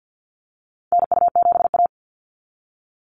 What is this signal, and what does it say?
Morse code, heard as a keyed tone, about 49 wpm, 700 Hz. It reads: NU7A
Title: NA SOTA Chaser Callsigns 50wpm